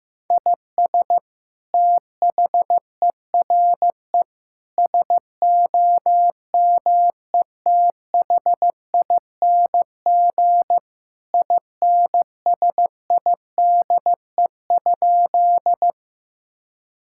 Morse code, heard as a keyed tone, about 15 wpm, 700 Hz. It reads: IS THERE SOMETHING INSIDE?
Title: Sentences from Top 1000 Words - Review Part 6 - 15wpm